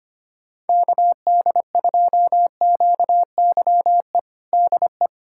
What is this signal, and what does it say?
Morse code, heard as a keyed tone, about 25 wpm, 700 Hz. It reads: KD2QYE DE
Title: USA Callsign DE Callsign 25wpm